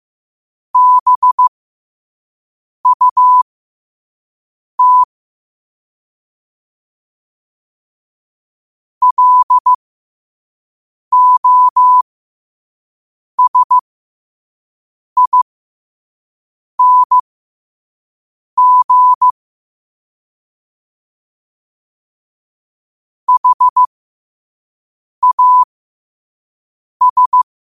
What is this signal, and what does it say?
Morse code, heard as a keyed tone, about 15 wpm, spaced as for 5 wpm, 1000 Hz. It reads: BUT LOSING HAS
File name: qotdfiles / 5wpm